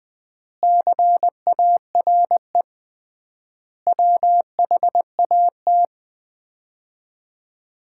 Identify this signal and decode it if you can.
Morse code, heard as a keyed tone, about 20 wpm, 700 Hz. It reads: CARE WHAT
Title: 3x Word-Spacing - Sets of 5 Words - 20wpm